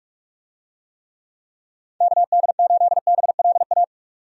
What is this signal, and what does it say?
Morse code, heard as a keyed tone, about 46 wpm, 700 Hz. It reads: KD8BLA
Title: USA Call Signs - 6 Characters - 45wpm